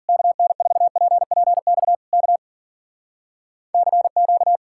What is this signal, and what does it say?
Morse code, heard as a keyed tone, about 52 wpm, 700 Hz. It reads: KN4PPX K CQ